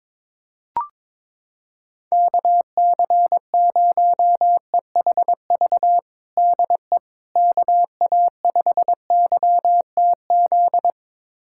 Morse code, heard as a keyed tone, about 21 wpm, 700 Hz. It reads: KC0EHV DE KA5YTZ
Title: USA Callsign DE Callsign - 6 Characters - 22wpm